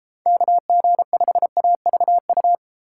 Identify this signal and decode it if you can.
Morse code, heard as a keyed tone, about 32 wpm, 700 Hz. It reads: KG5AVU